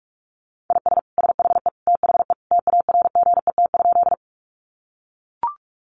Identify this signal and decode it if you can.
Morse code, heard as a keyed tone, about 44 wpm, 700 Hz. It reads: IS SHE THE TARGET?
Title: Sentences from Top 1200 Words 45wpm